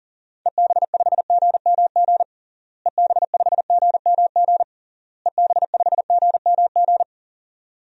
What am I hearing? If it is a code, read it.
Morse code, heard as a keyed tone, about 40 wpm, 700 Hz. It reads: EB5GMG EB5GMG EB5GMG
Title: DXCC-100 Call Signs - 6 Characters - Repeated 3x - 40wpm